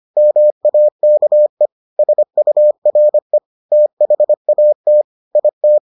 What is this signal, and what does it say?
Morse code, heard as a keyed tone, about 25 wpm, 600 Hz. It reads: MAKE SURE THAT IT